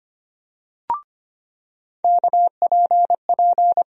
Morse code, heard as a keyed tone, about 25 wpm, 700 Hz. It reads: KPP
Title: USA Call Sign Postfixes - 3 Letters 25wpm